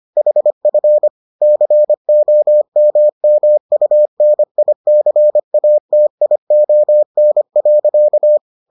Morse code, heard as a keyed tone, about 25 wpm, 600 Hz. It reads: HF COMMUNICATION.